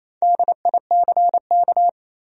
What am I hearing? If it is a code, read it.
Morse code, heard as a keyed tone, about 28 wpm, 700 Hz. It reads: DICK
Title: USA Common Names - 28wpm